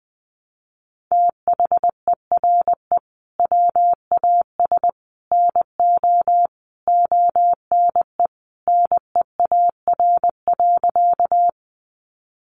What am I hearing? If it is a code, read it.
Morse code, heard as a keyed tone, about 20 wpm, 700 Hz. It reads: THERE WAS NO ONE NEAR.
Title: Sentences from Top 200 Words 20wpm